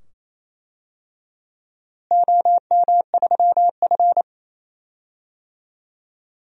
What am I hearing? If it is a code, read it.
Morse code, heard as a keyed tone, about 27 wpm, 700 Hz. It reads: OM3F
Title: DXCC-100 Call Signs - 4 Characters - Repeated 2x - 28wpm